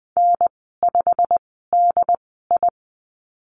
Morse code, heard as a keyed tone, about 20 wpm, spaced as for 15 wpm, 700 Hz. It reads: N5DI
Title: 088 Lesson 15g - Number 5 - Call Signs 20wpm